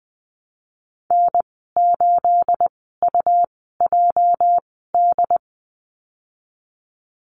Morse code, heard as a keyed tone, about 20 wpm, spaced as for 15 wpm, 700 Hz. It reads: N8UJD
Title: 238 Lesson 38g - Letter q - Call Signs 20wpm